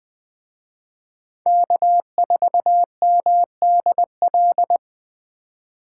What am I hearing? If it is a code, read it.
Morse code, heard as a keyed tone, about 21 wpm, 700 Hz. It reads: K4MDL